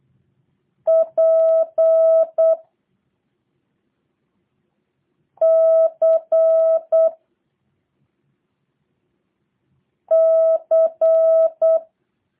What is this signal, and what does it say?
Morse code, heard as a keyed tone, about 8 wpm, 650 Hz. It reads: P C C
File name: jh8cbh / 2023cw / review1-5